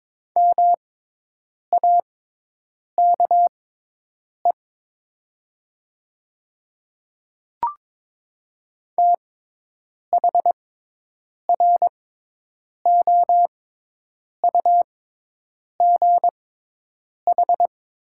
Morse code, heard as a keyed tone, about 22 wpm, spaced as for 8 wpm, 700 Hz. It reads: MAKE THROUGH
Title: 6x Character Spacing - Top 200 Words - 22wpm